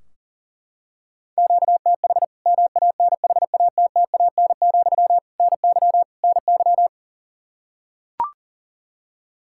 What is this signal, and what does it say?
Morse code, heard as a keyed tone, about 40 wpm, 700 Hz. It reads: QTH MANHATTAN, NY NY
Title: Practice QSO - Exchange 1 - Beginning - 40wpm